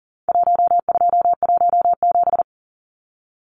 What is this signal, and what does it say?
Morse code, heard as a keyed tone, about 40 wpm, 700 Hz. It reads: 1217
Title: Numbers - Contest Style - 4-Digit 40wpm